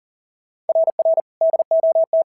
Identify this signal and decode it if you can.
Morse code, heard as a keyed tone, about 40 wpm, 650 Hz. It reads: RR DOT
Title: SOTA QSO Practice 40wpm